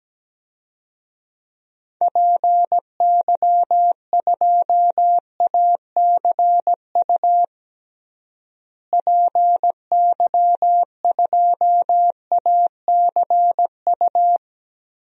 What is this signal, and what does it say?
Morse code, heard as a keyed tone, about 17 wpm, 700 Hz. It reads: PY2ACU PY2ACU
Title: DXCC-100 Call Signs - 6 Characters - Repeated 3x - 17wpm